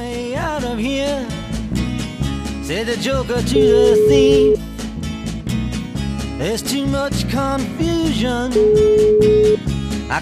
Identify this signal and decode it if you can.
Ringback tone (Europe, 425 Hz), with noise.